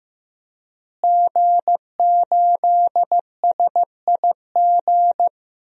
Morse code, heard as a keyed tone, about 15 wpm, 700 Hz. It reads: G8SIG
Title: DXCC-100 Call Signs - 5 Characters - Repeated 3x - 15wpm